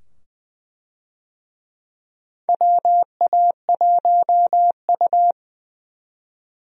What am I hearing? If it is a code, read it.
Morse code, heard as a keyed tone, about 20 wpm, 700 Hz. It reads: WA1U
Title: USA Call Signs - 4 Characters - 20wpm